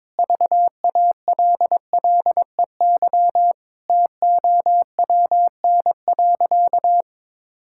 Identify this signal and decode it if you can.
Morse code, heard as a keyed tone, about 22 wpm, 700 Hz. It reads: VALLEY TOWN.
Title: Pangram Challenge v2 - 22wpm